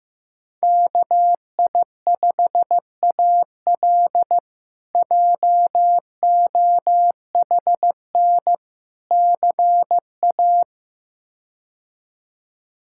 Morse code, heard as a keyed tone, about 15 wpm, 700 Hz. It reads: KI5AL JOHN CA
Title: K1USN SST QSO Practice - 15wpm